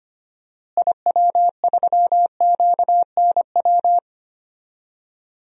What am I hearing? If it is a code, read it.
Morse code, heard as a keyed tone, about 25 wpm, 700 Hz. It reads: IW3QNW
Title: DXCC-100 Call Signs - 6 Characters - Repeated 3x - 25wpm